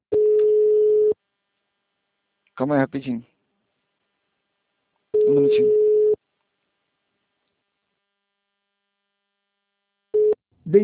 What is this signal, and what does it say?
Ringback tone (Europe, 425 Hz), with speech.